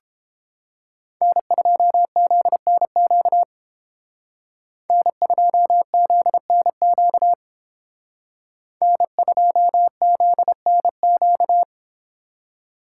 Morse code, heard as a keyed tone, about 30 wpm, 700 Hz. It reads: N2ZNQ N2ZNQ N2ZNQ